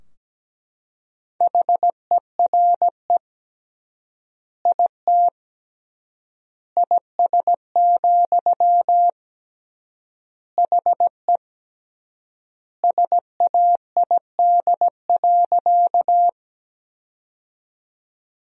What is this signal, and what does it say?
Morse code, heard as a keyed tone, about 17 wpm, 700 Hz. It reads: HERE IT IS, HE SAID.
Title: 3x Word-Spacing - Sentences from Top 200 Words - 17wpm